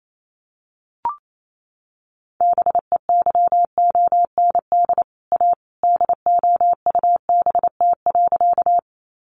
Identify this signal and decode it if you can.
Morse code, heard as a keyed tone, about 28 wpm, 700 Hz. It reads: BEYOND A DOUBT.
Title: Sentences from Top 1500 Words 28wpm